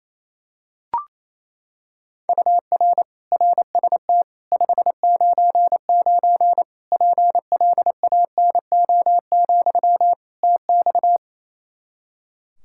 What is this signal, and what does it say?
Morse code, heard as a keyed tone, about 28 wpm, 700 Hz. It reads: UR RST 599 PLANO, TX
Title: K1USN SST QSO Practice 28wpm